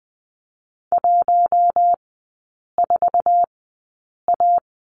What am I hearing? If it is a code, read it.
Morse code, heard as a keyed tone, about 20 wpm, spaced as for 8 wpm, 700 Hz. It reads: 14A
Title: 074 Lesson 13e - Letter L - Three Characters 20wpm